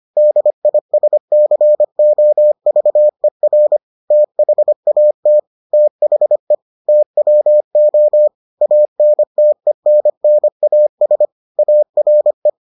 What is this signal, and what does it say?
Morse code, heard as a keyed tone, about 25 wpm, 600 Hz. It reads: DISCOVER THAT THE TWO ANTENNAS ARE